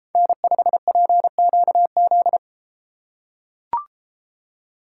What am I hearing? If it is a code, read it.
Morse code, heard as a keyed tone, about 31 wpm, 700 Hz. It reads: N5PQZ